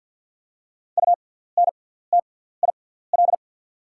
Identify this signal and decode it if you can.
Morse code, heard as a keyed tone, about 48 wpm, spaced as for 17 wpm, 690 Hz. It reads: UNTIL
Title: 6x Character Spacing - Top 300 Words - 50wpm